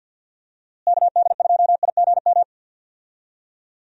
Morse code, heard as a keyed tone, about 51 wpm, 700 Hz. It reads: KD2IGM
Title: USA Call Signs - 6 Characters - Repeated 2x - 50wpm